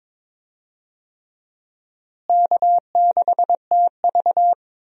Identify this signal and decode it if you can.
Morse code, heard as a keyed tone, about 21 wpm, 700 Hz. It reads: K6TV
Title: USA Call Signs - 4 Characters - Repeated 3x - 22wpm